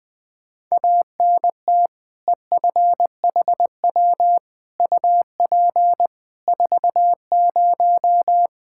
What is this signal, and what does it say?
Morse code, heard as a keyed tone, about 20 wpm, 700 Hz. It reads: ANT EFHW UP 40
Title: Practice QSO - Exchange 2 - Info - 20wpm